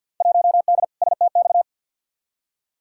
Morse code, heard as a keyed tone, about 52 wpm, 700 Hz. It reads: 1D STX
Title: Field Day Exchanges - 50wpm